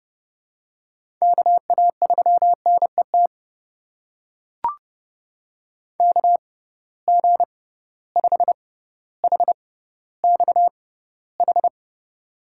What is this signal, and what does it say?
Morse code, heard as a keyed tone, about 30 wpm, spaced as for 15 wpm, 700 Hz. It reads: KA3NET KG5HXH
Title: USA Call Signs - 6 Characters - 6x Character Spacing - 30wpm